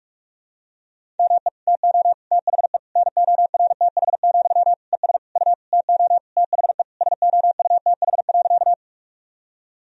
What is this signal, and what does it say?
Morse code, heard as a keyed tone, about 46 wpm, 700 Hz. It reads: ME TO THE NORTH, ES U TO THE SOUTH.
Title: Sentences from Top 400 Words and Abbreviations 45wpm